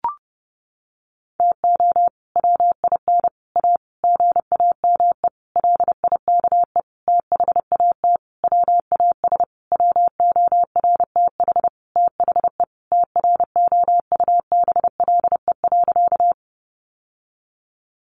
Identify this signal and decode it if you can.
Morse code, heard as a keyed tone, about 30 wpm, 700 Hz. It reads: TO WIN A GAME LIKE THAT WAS WORTH THE TROUBLE.
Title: Sentences from Top 1500 Words 30wpm